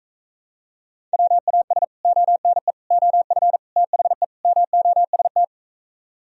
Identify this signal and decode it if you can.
Morse code, heard as a keyed tone, about 45 wpm, 700 Hz. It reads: WAS ONE OF THE MOST